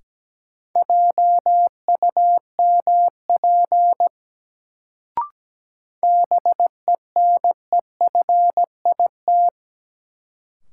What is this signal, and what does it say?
Morse code, heard as a keyed tone, about 17 wpm, 700 Hz. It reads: JUMP BENEFIT
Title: Sentences from Top 1300 Words - New Word Drill Alt - 17wpm